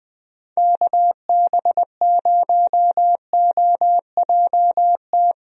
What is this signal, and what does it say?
Morse code, heard as a keyed tone, about 20 wpm, 700 Hz. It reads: KB0OJT
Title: USA Call Signs - 6 Characters - Repeated 2x - 20wpm